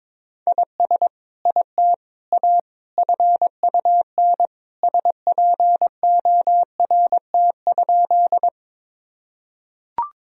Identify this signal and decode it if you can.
Morse code, heard as a keyed tone, about 22 wpm, 700 Hz. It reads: IS IT A FUN SPORT?
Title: Sentences from Top 1000 Words - Review Part 4 - 22wpm